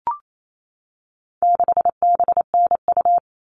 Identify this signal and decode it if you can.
Morse code, heard as a keyed tone, about 28 wpm, 700 Hz. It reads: 6BNU
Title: All Cut Numbers - 4-Digit 28wpm